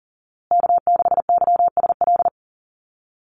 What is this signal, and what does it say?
Morse code, heard as a keyed tone, about 40 wpm, 700 Hz. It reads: K6YSL
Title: USA Call Signs Repeated 2x 40wpm